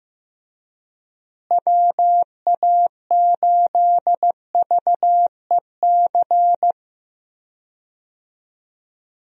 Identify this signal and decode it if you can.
Morse code, heard as a keyed tone, about 15 wpm, 700 Hz. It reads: WA8VEC
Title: USA Call Signs - 6 Characters - Repeated 2x - 15wpm